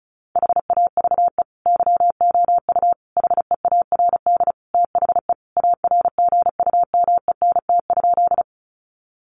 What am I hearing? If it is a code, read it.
Morse code, heard as a keyed tone, about 34 wpm, 700 Hz. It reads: HAVE YOU HEARD THE ARGUMENT?
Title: Sentences from Top 1400 Words 35wpm